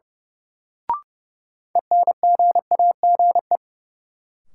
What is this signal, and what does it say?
Morse code, heard as a keyed tone, about 30 wpm, 700 Hz. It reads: ENGAGE
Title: Sentences from Top 1500 Words - New Word Drill Alt - 30wpm